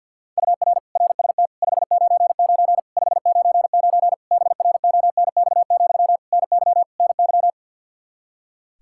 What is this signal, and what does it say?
Morse code, heard as a keyed tone, about 50 wpm, 700 Hz. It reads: UR RST 599 599 BRONX, NY NY
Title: Practice QSO - Exchange 1 - Beginning - 50wpm